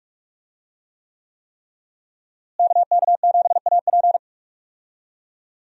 Morse code, heard as a keyed tone, about 44 wpm, 700 Hz. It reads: KK7AP
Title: USA Call Signs - 5 Characters - Repeated 3x - 45wpm